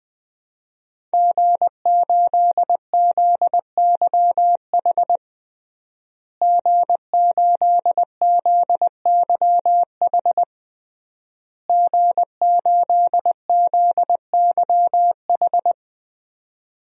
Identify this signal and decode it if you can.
Morse code, heard as a keyed tone, about 20 wpm, 700 Hz. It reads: G8ZYH G8ZYH G8ZYH
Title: DXCC-100 Call Signs - 5 Characters - Repeated 3x - 20wpm